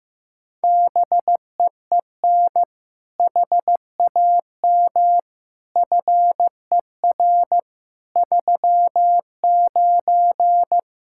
Morse code, heard as a keyed tone, about 15 wpm, 700 Hz. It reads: BEEN HAM FER 39
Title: K1USN SST QSO Practice 15wpm